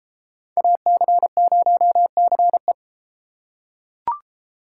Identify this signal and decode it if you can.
Morse code, heard as a keyed tone, about 32 wpm, 700 Hz. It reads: AC0CE